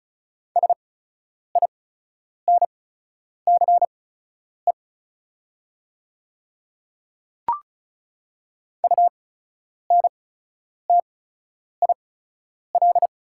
Morse code, heard as a keyed tone, about 36 wpm, spaced as for 10 wpm, 700 Hz. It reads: SINCE UNTIL
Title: 8x Character Spacing - Top 300 Words - 35wpm